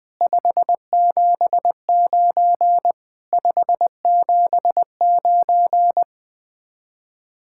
Morse code, heard as a keyed tone, about 20 wpm, 700 Hz. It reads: 579 579
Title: Speed-Racing Common QSO Phrases - 17wpm